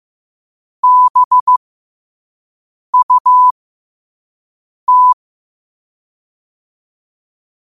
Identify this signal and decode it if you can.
Morse code, heard as a keyed tone, about 15 wpm, spaced as for 5 wpm, 1000 Hz. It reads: BUT
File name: 5wpm